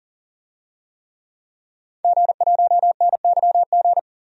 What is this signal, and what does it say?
Morse code, heard as a keyed tone, about 40 wpm, 700 Hz. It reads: G1NYG
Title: DXCC-100 Call Signs - 5 Characters - 40wpm